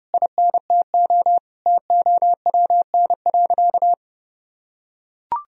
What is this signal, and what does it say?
Morse code, heard as a keyed tone, about 30 wpm, 700 Hz. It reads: INTO TOWN.
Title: Sentences from Top 1000 Words - Review Part 2 - 30wpm